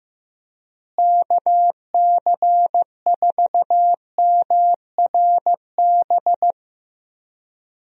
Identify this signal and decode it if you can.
Morse code, heard as a keyed tone, about 15 wpm, 700 Hz. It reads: KC4MRB
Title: USA Call Signs - 6 Characters - Repeated 2x - 15wpm